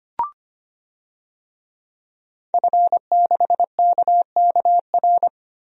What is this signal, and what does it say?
Morse code, heard as a keyed tone, about 25 wpm, 700 Hz. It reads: F6KKR
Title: DXCC-100 Call Signs - 5 Characters - 25wpm